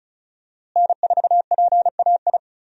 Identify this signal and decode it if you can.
Morse code, heard as a keyed tone, about 36 wpm, 700 Hz. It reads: N4PAI
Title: USA Call Signs - 5 Characters - Repeated 2x - 35wpm